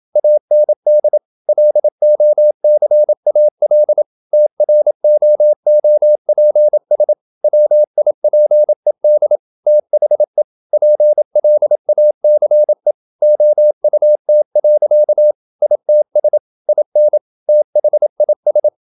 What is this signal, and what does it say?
Morse code, heard as a keyed tone, about 27 wpm, 600 Hz. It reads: AND LOCAL TROOPS WIPED THE PLACE OUT. ITS IN THIS
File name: Never 0027wpm